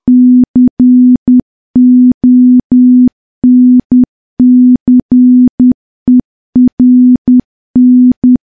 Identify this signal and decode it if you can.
Morse code, heard as a keyed tone, about 10 wpm, 250 Hz. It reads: CONCERN
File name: morse / morse-training-10WPM